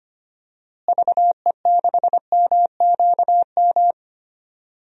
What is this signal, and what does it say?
Morse code, heard as a keyed tone, about 25 wpm, 700 Hz. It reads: VE6MQM
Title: DXCC-100 Call Signs - 6 Characters - Repeated 2x - 25wpm